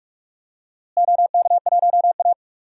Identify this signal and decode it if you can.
Morse code, heard as a keyed tone, about 46 wpm, 700 Hz. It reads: OK1A
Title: DXCC-100 Call Signs - 4 Characters - Repeated 2x - 45wpm